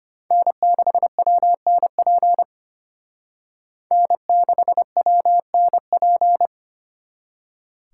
Morse code, heard as a keyed tone, about 33 wpm, 700 Hz. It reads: N6WNP N6WNP